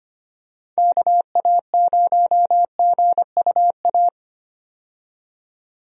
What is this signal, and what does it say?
Morse code, heard as a keyed tone, about 25 wpm, 700 Hz. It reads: KA0GUA
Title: USA Call Signs - 6 Characters - 25wpm